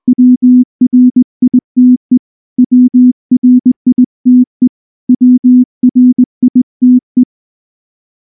Morse code, heard as a keyed tone, about 20 wpm, 250 Hz. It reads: WRITE WRITE WRITE
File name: morse / morse-training-20WPM-10